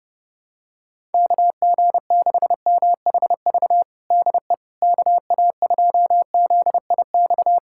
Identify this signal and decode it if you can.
Morse code, heard as a keyed tone, about 30 wpm, 700 Hz. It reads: KG6MHV DE KA2ZIX
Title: USA Callsign DE Callsign 30wpm